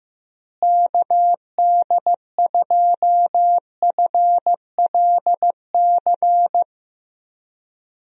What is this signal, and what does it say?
Morse code, heard as a keyed tone, about 15 wpm, 700 Hz. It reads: KD2FLC
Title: USA Call Signs - 6 Characters - Repeated 2x - 15wpm